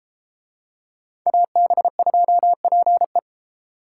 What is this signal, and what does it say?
Morse code, heard as a keyed tone, about 30 wpm, 700 Hz. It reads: AB2PE